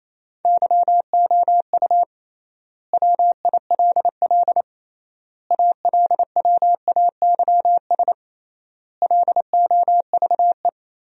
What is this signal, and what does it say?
Morse code, heard as a keyed tone, about 28 wpm, 700 Hz. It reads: YOU WILL ALWAYS LOVE
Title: 3x Word-Spacing - Sentences from Top 400 Words - 28wpm